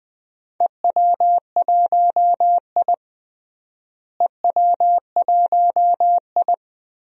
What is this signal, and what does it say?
Morse code, heard as a keyed tone, about 20 wpm, 700 Hz. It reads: EW1I EW1I
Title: DXCC-100 Call Signs - 4 Characters - Repeated 3x - 20wpm